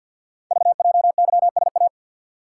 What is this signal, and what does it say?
Morse code, heard as a keyed tone, about 48 wpm, 700 Hz. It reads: VJYSA
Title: Code Groups - 5 Letters - 50wpm